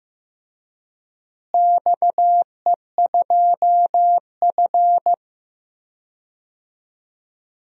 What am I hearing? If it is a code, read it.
Morse code, heard as a keyed tone, about 15 wpm, 700 Hz. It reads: XE2F
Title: DXCC-100 Call Signs - 4 Characters - Repeated 3x - 15wpm